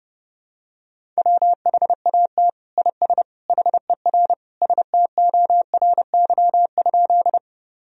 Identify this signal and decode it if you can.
Morse code, heard as a keyed tone, about 30 wpm, 700 Hz. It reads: WHAT IS HER STORY?
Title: Sentences from Top 1000 Words - Review Part 1 - 30wpm